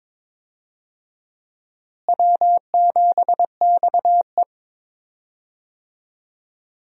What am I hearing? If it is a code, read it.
Morse code, heard as a keyed tone, about 22 wpm, 700 Hz. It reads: W7XE
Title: USA Call Signs - 4 Characters - Repeated 3x - 22wpm